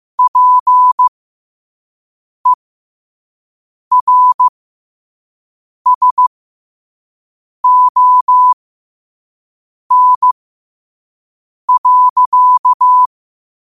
Morse code, heard as a keyed tone, about 15 wpm, spaced as for 5 wpm, 1000 Hz. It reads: PERSON.
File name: qotdfiles / 5wpm